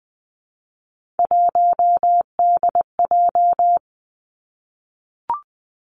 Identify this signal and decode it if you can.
Morse code, heard as a keyed tone, about 20 wpm, 700 Hz. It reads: 1DJ
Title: Three Challenging Letter-Number Combinations 20wpm